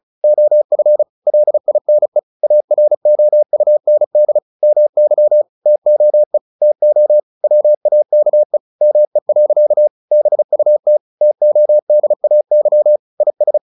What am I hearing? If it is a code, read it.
Morse code, heard as a keyed tone, about 34 wpm, 600 Hz. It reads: OF LINE AROUND MY TOE TO WAKE ME. BUT TODAY IS
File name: the old man and the sea / the old man and the sea 1 35wpm